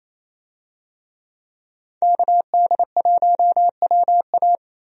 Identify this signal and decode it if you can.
Morse code, heard as a keyed tone, about 28 wpm, 700 Hz. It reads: KD1WA
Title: USA Call Signs - 5 Characters - 28wpm